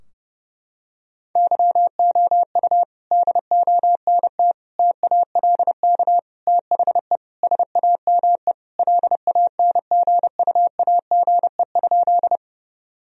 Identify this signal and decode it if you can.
Morse code, heard as a keyed tone, about 30 wpm, 700 Hz. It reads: YOU DONT TALK THE SAME LANGUAGE?
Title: Sentences from Top 1000 Words - Review Part 3 - 30wpm